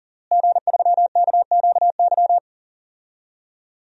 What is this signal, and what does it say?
Morse code, heard as a keyed tone, about 40 wpm, 700 Hz. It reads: G3KQY
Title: DXCC-100 Call Signs - 5 Characters - Repeated 3x - 40wpm